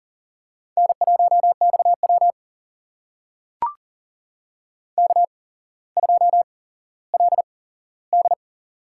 Morse code, heard as a keyed tone, about 42 wpm, spaced as for 17 wpm, 700 Hz. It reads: N1XW K2LD
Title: USA Call Signs - 4 Characters - 8x Character Spacing - 40wpm